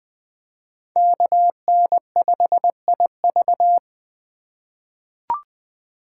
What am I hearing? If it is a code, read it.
Morse code, heard as a keyed tone, about 20 wpm, 700 Hz. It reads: KN5IV
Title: Code Groups - 5 Letter:Numbers - 20wpm